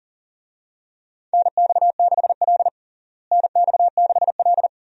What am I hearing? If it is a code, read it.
Morse code, heard as a keyed tone, about 40 wpm, 700 Hz. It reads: NX6L NX6L
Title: USA Call Signs - 4 Characters - Repeated 2x - 40wpm